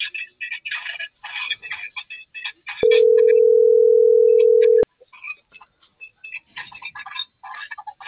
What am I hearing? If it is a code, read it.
Ringback tone (North America), 440 + 480 Hz.